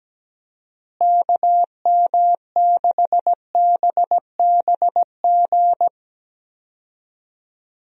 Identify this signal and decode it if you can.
Morse code, heard as a keyed tone, about 17 wpm, 700 Hz. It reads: KM6BBG
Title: USA Call Signs - 6 Characters - Repeated 2x - 17wpm